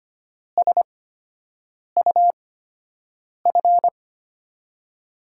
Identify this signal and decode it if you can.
Morse code, heard as a keyed tone, about 25 wpm, spaced as for 7 wpm, 700 Hz. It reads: SUF
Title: USA Call Signs - 5 Characters - 8x Character Spacing - 25wpm